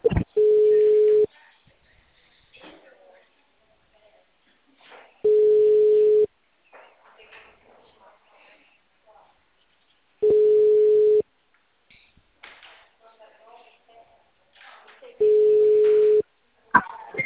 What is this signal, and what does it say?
Ringback tone (Europe, 425 Hz), with speech.